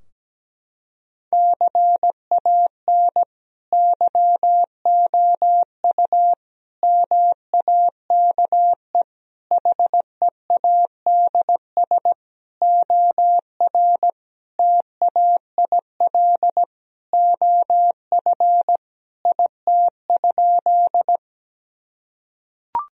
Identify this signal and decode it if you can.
Morse code, heard as a keyed tone, about 17 wpm, 700 Hz. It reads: CAN YOU MAKE HEADS OR TAIL OF IT?
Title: Sentences from Top 1000 Words - Review Part 3 - 17wpm